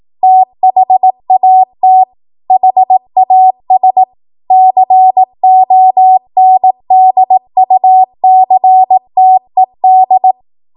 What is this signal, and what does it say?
Morse code, heard as a keyed tone, about 18 wpm, 750 Hz. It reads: THAT HAS CONDUCTED